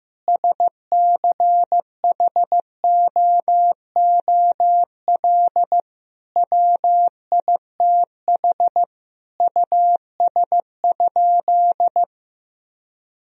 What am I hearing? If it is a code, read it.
Morse code, heard as a keyed tone, about 15 wpm, 700 Hz. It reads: SCHOOL WITH US?
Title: Sentences from Top 1000 Words - Review Part 3 - 15wpm